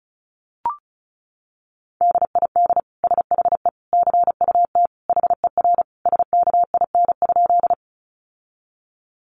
Morse code, heard as a keyed tone, about 34 wpm, 700 Hz. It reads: DID SHE CUT HER SKIN?
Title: Sentences from Top 1300 Words 35wpm